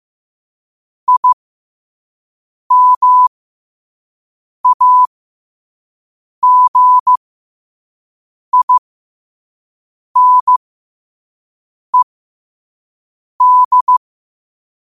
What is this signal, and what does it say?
Morse code, heard as a keyed tone, about 15 wpm, spaced as for 5 wpm, 1000 Hz. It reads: IMAGINED
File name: qotdfiles / 5wpm